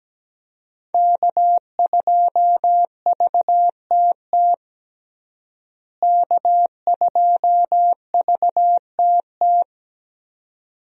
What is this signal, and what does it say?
Morse code, heard as a keyed tone, about 17 wpm, 700 Hz. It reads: K2VTT K2VTT
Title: USA Call Signs - 5 Characters - Repeated 2x - 17wpm